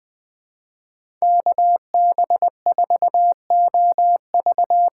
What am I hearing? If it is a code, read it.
Morse code, heard as a keyed tone, about 20 wpm, 700 Hz. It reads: KB4OV
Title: USA Call Signs - 6 Characters - 6x Character Spacing - 20wpm